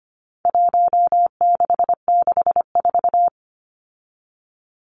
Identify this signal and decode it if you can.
Morse code, heard as a keyed tone, about 25 wpm, 700 Hz. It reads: 1664
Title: Numbers - Contest Style - 4-Digit 25wpm